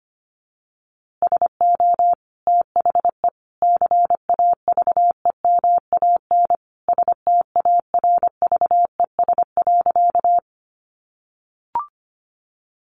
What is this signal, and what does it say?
Morse code, heard as a keyed tone, about 25 wpm, 700 Hz. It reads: SO THE CAVEMAN STARVES.
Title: A Sound Of Thunder 25wpm